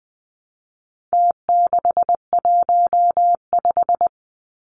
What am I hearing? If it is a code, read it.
Morse code, heard as a keyed tone, about 20 wpm, 700 Hz. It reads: T615
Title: Numbers - Contest Style - 4-Digit 20wpm